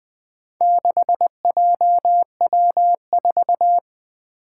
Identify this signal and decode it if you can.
Morse code, heard as a keyed tone, about 20 wpm, 700 Hz. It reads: 6JW4
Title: Four Challenging Letter-Number Combinations 20wpm